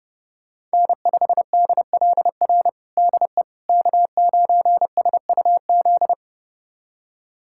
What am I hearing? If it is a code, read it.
Morse code, heard as a keyed tone, about 31 wpm, 700 Hz. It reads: N5DLR DE K9SUZ
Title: USA Callsign DE Callsign - 5 Characters - 30wpm